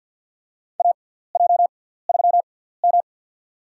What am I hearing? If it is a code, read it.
Morse code, heard as a keyed tone, about 49 wpm, spaced as for 17 wpm, 700 Hz. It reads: AJ3M
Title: USA Call Signs - 4 Characters - 6x Character Spacing - 50wpm